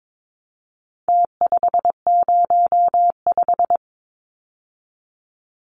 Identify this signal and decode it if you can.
Morse code, heard as a keyed tone, about 22 wpm, 700 Hz. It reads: T505
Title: Numbers - Contest Style - 4-Digit 22wpm